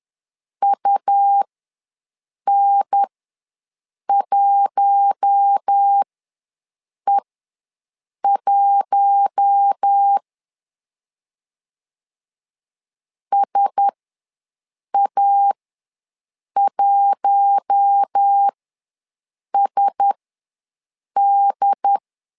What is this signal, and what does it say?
Morse code, heard as a keyed tone, about 10 wpm, spaced as for 6 wpm, 800 Hz. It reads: UN1E1 SA1SD
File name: LEZIONI / 4 Q1S ws 10wpm 2208